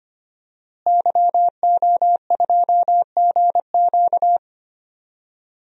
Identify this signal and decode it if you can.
Morse code, heard as a keyed tone, about 25 wpm, 700 Hz. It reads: YO2GQ
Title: DXCC-100 Call Signs - 5 Characters - Repeated 2x - 25wpm